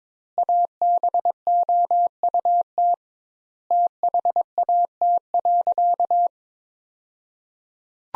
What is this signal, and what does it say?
Morse code, heard as a keyed tone, about 22 wpm, 700 Hz. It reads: ABOUT THAT.
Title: Sentences Top 200 Words - 22wpm